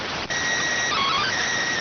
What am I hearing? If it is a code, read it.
SSTV (slow-scan television), PD 290.